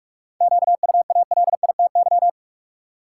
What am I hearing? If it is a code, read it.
Morse code, heard as a keyed tone, about 46 wpm, 700 Hz. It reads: QUALITY
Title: Seven Letter Words - 45wpm